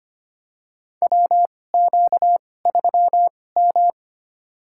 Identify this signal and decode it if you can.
Morse code, heard as a keyed tone, about 25 wpm, spaced as for 18 wpm, 700 Hz. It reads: WQ3M
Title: USA Call Signs - 4 Characters - 2x Character Spacing - 25wpm